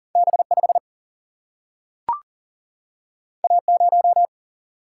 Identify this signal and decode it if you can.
Morse code, heard as a keyed tone, about 40 wpm, 700 Hz. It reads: B 5 A 0
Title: Single-Letter Single-Number 40wpm